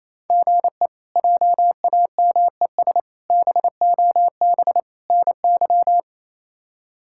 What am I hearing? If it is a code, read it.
Morse code, heard as a keyed tone, about 27 wpm, 700 Hz. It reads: GE JAMES BOB NY
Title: K1USN SST QSO Practice - 28wpm